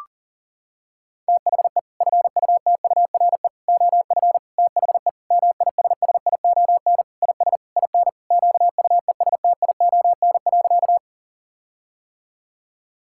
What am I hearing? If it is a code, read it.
Morse code, heard as a keyed tone, about 40 wpm, 700 Hz. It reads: THE FUTURE OF THE MISSION IS IN QUESTION.
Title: Sentences from Top 1600 Words 40wpm